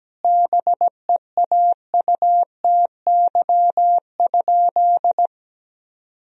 Morse code, heard as a keyed tone, about 17 wpm, 700 Hz. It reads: BEAUTY?
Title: Sentences from Top 1000 Words - Review Part 2 - 17wpm